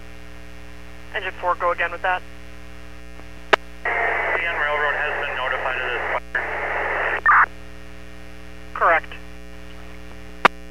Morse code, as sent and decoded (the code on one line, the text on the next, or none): none